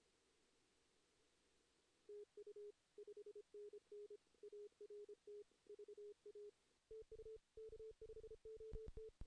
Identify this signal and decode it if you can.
Morse code, heard as a keyed tone, about 28 wpm, 440 Hz. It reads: TU 5NN ART VA TU K5OT
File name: 2016 JAN CW NAQP / 2016 Jan naqp k5ot - 0-3